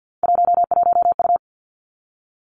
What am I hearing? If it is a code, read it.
Morse code, heard as a keyed tone, about 50 wpm, 700 Hz. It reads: 11U
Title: Three Challenging Letter-Number Combinations 50wpm